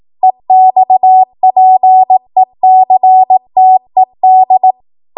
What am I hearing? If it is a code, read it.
Morse code, heard as a keyed tone, about 18 wpm, 750 Hz. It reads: EXPECTED
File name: Morse / 18WPM